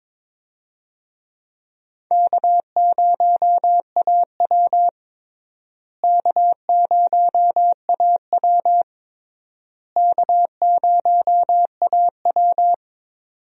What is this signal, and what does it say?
Morse code, heard as a keyed tone, about 22 wpm, 700 Hz. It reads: K0AW K0AW K0AW
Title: USA Call Signs - 4 Characters - Repeated 3x - 22wpm